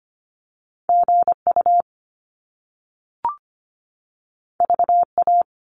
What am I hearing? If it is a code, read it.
Morse code, heard as a keyed tone, about 25 wpm, 700 Hz. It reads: GU VA
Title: Top Two Letter Combinations in Words 25wpm